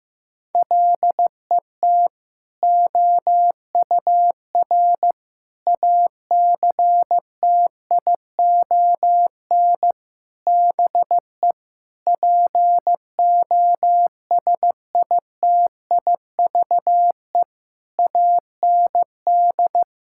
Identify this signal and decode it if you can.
Morse code, heard as a keyed tone, about 16 wpm, 700 Hz. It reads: LET OUR ACTION BE POSITIVE AND
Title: Sentences from Top 1000 Words - Review Part 6 - 15wpm